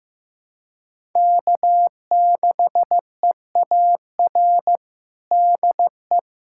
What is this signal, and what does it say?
Morse code, heard as a keyed tone, about 15 wpm, 700 Hz. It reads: K6EAR DE
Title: USA Callsign DE Callsign - 5 Characters - 15wpm